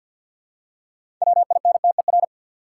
Morse code, heard as a keyed tone, about 46 wpm, 700 Hz. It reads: WINTER
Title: Six Letter Words - 50wpm